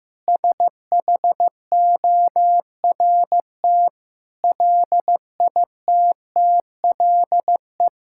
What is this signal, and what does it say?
Morse code, heard as a keyed tone, about 16 wpm, 700 Hz. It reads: SHORT LITTLE
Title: Sentences from Top 1600 Words 15wpm